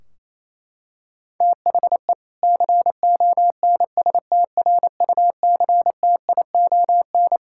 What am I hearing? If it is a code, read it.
Morse code, heard as a keyed tone, about 28 wpm, 700 Hz. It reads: THE CONSTRUCTION
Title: Sentences from Top 1600 Words 28wpm